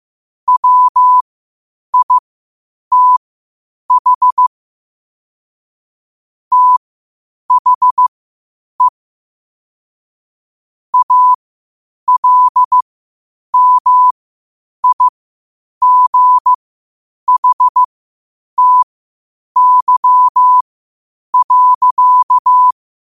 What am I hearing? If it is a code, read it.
Morse code, heard as a keyed tone, about 15 wpm, spaced as for 8 wpm, 1000 Hz. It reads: WITH THE ALMIGHTY.
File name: qotdfiles / 8wpm